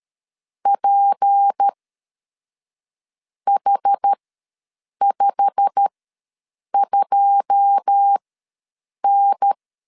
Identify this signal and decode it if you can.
Morse code, heard as a keyed tone, about 13 wpm, spaced as for 6 wpm, 800 Hz. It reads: PH52N
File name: LEZIONI / 9 286 ws 12wpm 3351